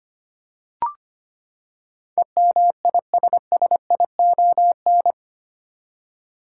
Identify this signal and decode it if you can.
Morse code, heard as a keyed tone, about 25 wpm, 700 Hz. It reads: EMISSION
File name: Eight Letter Words - 25wpm